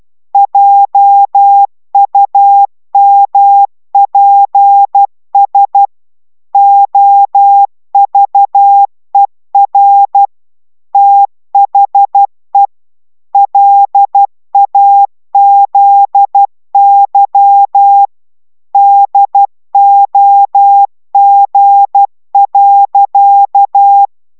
Morse code, heard as a keyed tone, about 12 wpm, 800 Hz. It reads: JUMPS OVER THE LAZY DOG.
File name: CW 10 baud